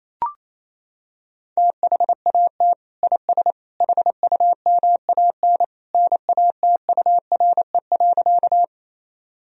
Morse code, heard as a keyed tone, about 28 wpm, 700 Hz. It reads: THAT IS HUMAN NATURE.
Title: Sentences from Top 1000 Words - Review Part 3 - 28wpm